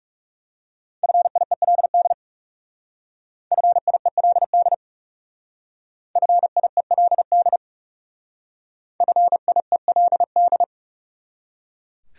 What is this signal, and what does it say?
Morse code, heard as a keyed tone, about 37 wpm, 700 Hz. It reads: FIELD FIELD FIELD FIELD
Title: Speed-Racing Top 500 Words 30wpm